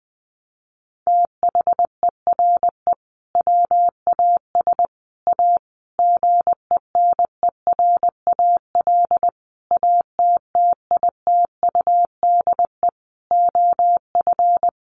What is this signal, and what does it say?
Morse code, heard as a keyed tone, about 20 wpm, 700 Hz. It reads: THERE WAS A GENERAL ATTITUDE OF
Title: Sentences from Top 1300 Words 20wpm